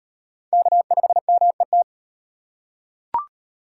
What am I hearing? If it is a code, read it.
Morse code, heard as a keyed tone, about 38 wpm, 700 Hz. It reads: K5MET